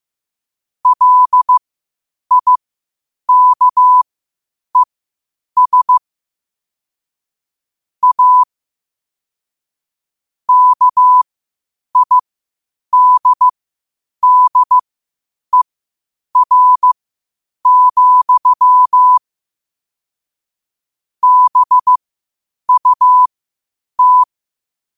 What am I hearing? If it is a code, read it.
Morse code, heard as a keyed tone, about 15 wpm, spaced as for 8 wpm, 1000 Hz. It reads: LIKES A KIDDER, BUT
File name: qotdfiles / 8wpm